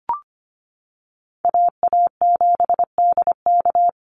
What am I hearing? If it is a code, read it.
Morse code, heard as a keyed tone, about 25 wpm, 700 Hz. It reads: AA7DK
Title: NA SOTA Chaser Callsigns 25wpm